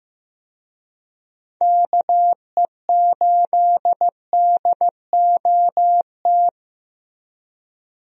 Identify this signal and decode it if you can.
Morse code, heard as a keyed tone, about 15 wpm, 700 Hz. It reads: KE8DOT
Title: USA Call Signs - 6 Characters - Repeated 3x - 15wpm